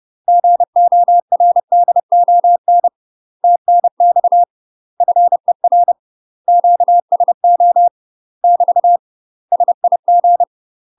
Morse code, heard as a keyed tone, about 30 wpm, 700 Hz. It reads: GORDON TNX FER QSO <BT> SIG
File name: UK MASTER QSOs TRAINING / 30WPM / UK MASTER QSOs 0009